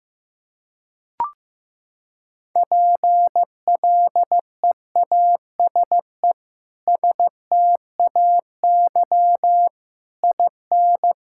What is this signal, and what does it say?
Morse code, heard as a keyed tone, about 16 wpm, 700 Hz. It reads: PLEASE STAY IN
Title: Sentences from Top 1600 Words 15wpm